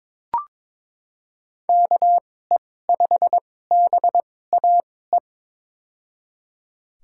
Morse code, heard as a keyed tone, about 22 wpm, spaced as for 16 wpm, 700 Hz. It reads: KE5BAE
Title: USA Call Signs - 6 Characters - 2x Character Spacing - 22wpm